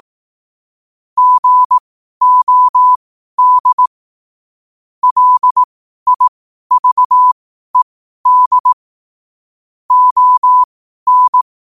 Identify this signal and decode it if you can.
Morse code, heard as a keyed tone, about 18 wpm, spaced as for 12 wpm, 1000 Hz. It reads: GOD LIVED ON